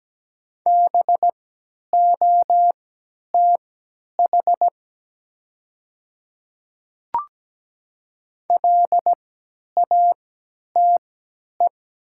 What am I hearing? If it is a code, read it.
Morse code, heard as a keyed tone, about 17 wpm, spaced as for 10 wpm, 700 Hz. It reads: BOTH LATE
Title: 3x Character Spacing - Top 500 Words - 17wpm